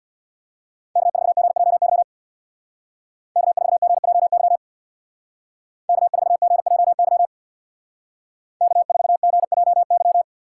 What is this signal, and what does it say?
Morse code, heard as a keyed tone, about 61 wpm, 700 Hz. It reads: K4GJY K4GJY K4GJY K4GJY
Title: Speed Racing - USA 1x3 Call Signs 50wpm